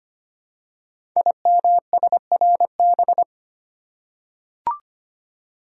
Morse code, heard as a keyed tone, about 25 wpm, 700 Hz. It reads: IMSRB
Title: Code Groups - 5 Letters - 25wpm